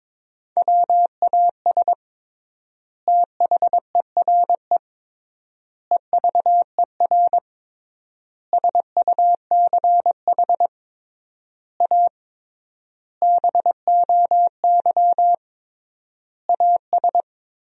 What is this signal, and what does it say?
Morse code, heard as a keyed tone, about 22 wpm, 700 Hz. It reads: WAS THERE EVER SUCH A BOY AS
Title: 3x Word-Spacing - Sentences from Top 400 Words - 22wpm